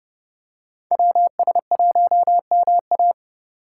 Morse code, heard as a keyed tone, about 31 wpm, 700 Hz. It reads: WS1MA
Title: Speed Racing - USA Call Signs 25wpm